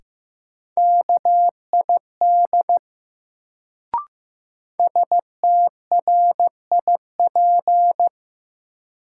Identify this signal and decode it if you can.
Morse code, heard as a keyed tone, about 15 wpm, 700 Hz. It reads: KID STRIP
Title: Sentences from Top 1600 Words - New Word Drill - 15wpm